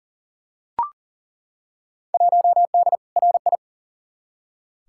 Morse code, heard as a keyed tone, about 39 wpm, 700 Hz. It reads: 1D RI
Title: Field Day Exchanges - 40wpm